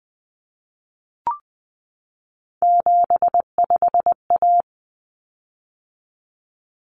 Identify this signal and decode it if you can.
Morse code, heard as a keyed tone, about 20 wpm, 700 Hz. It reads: 75A
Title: Common Cut Numbers - 3-Digit 20wpm